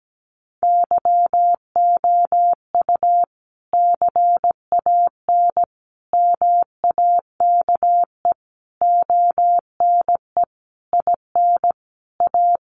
Morse code, heard as a keyed tone, about 17 wpm, 700 Hz. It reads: YOU CAN MAKE ONE IN A
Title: Sentences from Top 1000 Words 17wpm